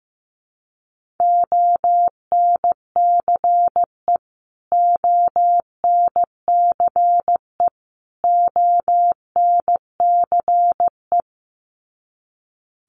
Morse code, heard as a keyed tone, about 15 wpm, 700 Hz. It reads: ONCE ONCE ONCE
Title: Top 500 Words Repeated 15wpm